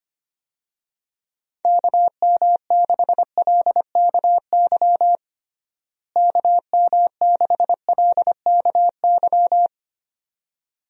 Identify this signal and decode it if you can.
Morse code, heard as a keyed tone, about 25 wpm, 700 Hz. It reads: KM6LKY KM6LKY
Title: USA Call Signs - 6 Characters - Repeated 2x - 25wpm